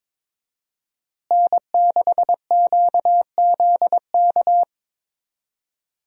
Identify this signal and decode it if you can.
Morse code, heard as a keyed tone, about 22 wpm, 700 Hz. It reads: N6QZK
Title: USA Call Signs - 5 Characters - Repeated 3x - 22wpm